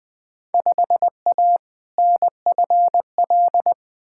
Morse code, heard as a keyed tone, about 20 wpm, 700 Hz. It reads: 5A NFL
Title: Field Day Exchanges - 20wpm